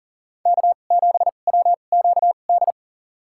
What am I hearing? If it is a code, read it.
Morse code, heard as a keyed tone, about 37 wpm, spaced as for 28 wpm, 700 Hz. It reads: K7WQD
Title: USA Call Signs - 5 Characters - 2x Character Spacing - 40wpm